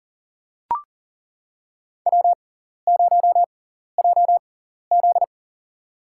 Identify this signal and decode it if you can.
Morse code, heard as a keyed tone, about 40 wpm, spaced as for 14 wpm, 700 Hz. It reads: W0JZ
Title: USA Call Signs - 6x Character Spacing - 40wpm